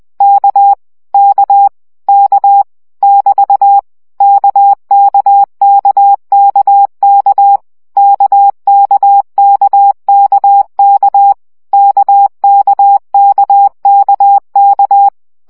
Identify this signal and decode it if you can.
Morse code, heard as a keyed tone, about 20 wpm, 800 Hz. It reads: K K K <BT> KKKKK KKKKK KKKKK
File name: CW gyakorlas k 17 baud